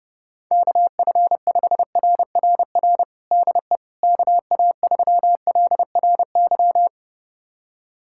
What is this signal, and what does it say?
Morse code, heard as a keyed tone, about 30 wpm, 700 Hz. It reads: KF5RRR DE KA3LRY
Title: USA Callsign DE Callsign - 6 Characters - 30wpm